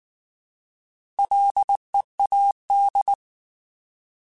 Morse code, heard as a keyed tone, about 19 wpm, 800 Hz. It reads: LEAD
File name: Alphabetical / Alphabetical mp3s / Ls @18